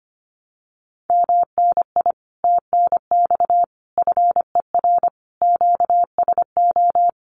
Morse code, heard as a keyed tone, about 25 wpm, 700 Hz. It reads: MNI TNX FER QSO
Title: Common QSO Phrases Encore 25wpm